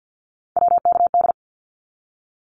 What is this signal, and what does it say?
Morse code, heard as a keyed tone, about 50 wpm, 700 Hz. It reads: WKD
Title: Abbreviations 50wpm